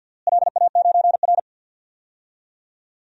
Morse code, heard as a keyed tone, about 53 wpm, 700 Hz. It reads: LA9R
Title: DXCC-100 Call Signs - 4 Characters - Repeated 2x - 50wpm